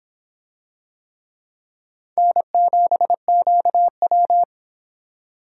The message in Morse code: N7QW

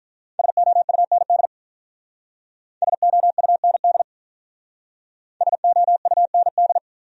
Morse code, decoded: SOUND SOUND SOUND

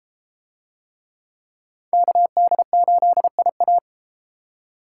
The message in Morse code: KD8IA

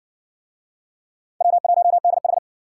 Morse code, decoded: W1DH

